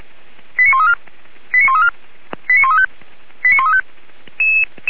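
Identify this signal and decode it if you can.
Five-tone selective call (ZVEI), heard as quick runs of five tones, tones of about 70 ms. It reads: 89134 89134 89135 89135
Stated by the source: Five-tone ZVEI selective call 89132, received by SDR, feuerwehr-orlamuende fire brigade.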